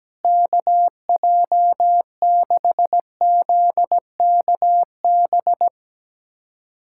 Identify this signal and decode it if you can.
Morse code, heard as a keyed tone, about 17 wpm, 700 Hz. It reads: KJ6ZKB
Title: USA Callsign DE Callsign 17wpm